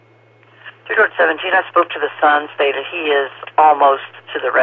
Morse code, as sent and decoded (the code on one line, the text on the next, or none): none